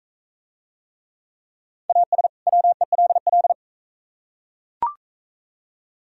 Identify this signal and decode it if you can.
Morse code, heard as a keyed tone, about 41 wpm, 700 Hz. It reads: AS WELL